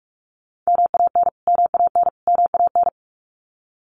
Morse code, heard as a keyed tone, about 44 wpm, 700 Hz. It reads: MAN MAN MAN
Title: Top 300 Words Repeated 45wpm